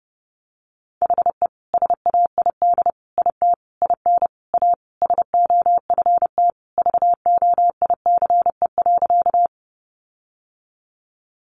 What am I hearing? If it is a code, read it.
Morse code, heard as a keyed tone, about 30 wpm, 700 Hz. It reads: HE SAID IT IN A SOFT VOICE.
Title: Sentences from Top 1500 Words 30wpm